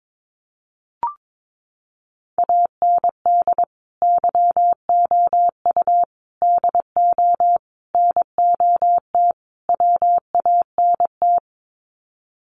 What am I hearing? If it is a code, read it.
Morse code, heard as a keyed tone, about 22 wpm, 700 Hz. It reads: AND YOU DO NOT WANT
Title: Sets of 5 Words Encore2 22wpm